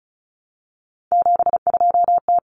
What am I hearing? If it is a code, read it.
Morse code, heard as a keyed tone, about 34 wpm, 700 Hz. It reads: 72T
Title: Common Cut Numbers - 3-Digit 35wpm